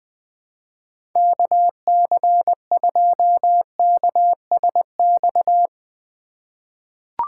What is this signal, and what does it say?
Morse code, heard as a keyed tone, about 20 wpm, 700 Hz. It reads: KC2KSX
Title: USA Call Signs - 6 Characters - 3x Character Spacing - 20wpm